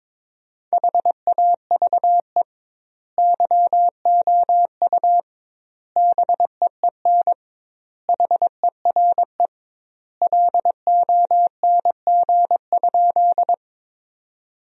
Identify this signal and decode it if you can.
Morse code, heard as a keyed tone, about 22 wpm, 700 Hz. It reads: HAVE YOU BEEN HERE LONG?
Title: 2x Word-Spacing - Sentences from Top 200 Words - 22wpm